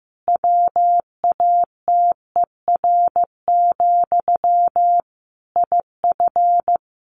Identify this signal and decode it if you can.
Morse code, heard as a keyed tone, about 15 wpm, 700 Hz. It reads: WATER, IF
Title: Sentences from Top 1500 Words 15wpm